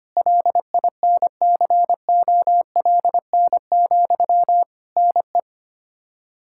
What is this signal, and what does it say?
Morse code, heard as a keyed tone, about 25 wpm, 700 Hz. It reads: LINCOLN, NE
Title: Practice QSO - Exchange 1 - Beginning - 25wpm